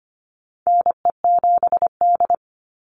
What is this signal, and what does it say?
Morse code, heard as a keyed tone, about 25 wpm, 700 Hz. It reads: NE7D
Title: Call Signs - 25wpm